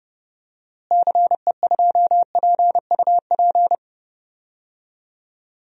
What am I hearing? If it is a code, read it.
Morse code, heard as a keyed tone, about 31 wpm, 700 Hz. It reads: CE2PUP
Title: DXCC-100 Call Signs - 6 Characters - Repeated 3x - 30wpm